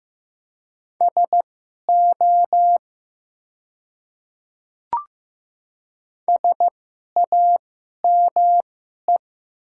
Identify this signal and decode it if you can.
Morse code, heard as a keyed tone, about 15 wpm, spaced as for 11 wpm, 700 Hz. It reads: SO SAME
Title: 2x Character Spacing - Top 200 Words - 15wpm